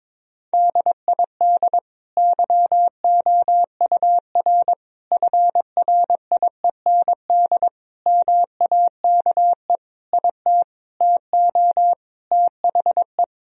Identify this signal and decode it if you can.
Morse code, heard as a keyed tone, about 22 wpm, 700 Hz. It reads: DID YOUR FRIEND MAKE IT TO THE
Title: Sentences from Top 1000 Words - Review Part 3 - 22wpm